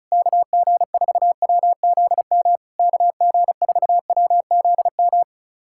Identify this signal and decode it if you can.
Morse code, heard as a keyed tone, about 36 wpm, 700 Hz. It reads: KG4WZM KG4WZM
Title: SOTA QSO Practice 35wpm